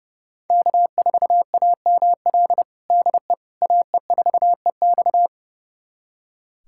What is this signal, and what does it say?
Morse code, heard as a keyed tone, about 32 wpm, 700 Hz. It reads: K4AML DE AE4EX
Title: USA Callsign DE Callsign - 5 Characters - 30wpm